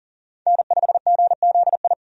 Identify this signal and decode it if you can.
Morse code, heard as a keyed tone, about 37 wpm, 700 Hz. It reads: N5GZI
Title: USA Call Signs - 5 Characters - 40wpm